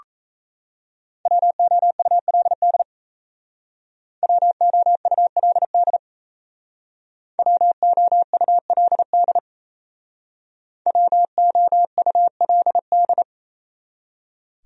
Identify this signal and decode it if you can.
Morse code, heard as a keyed tone, about 36 wpm, 700 Hz. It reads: WOULD WOULD WOULD WOULD